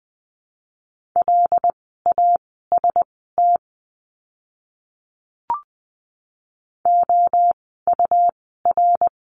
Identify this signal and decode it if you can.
Morse code, heard as a keyed tone, about 20 wpm, spaced as for 14 wpm, 700 Hz. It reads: LAST OUR
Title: 101 Lesson 17f - Letter u - Words 20wpm